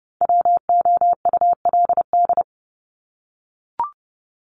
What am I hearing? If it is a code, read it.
Morse code, heard as a keyed tone, about 30 wpm, 700 Hz. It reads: WOULD
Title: Top 200 Words 30wpm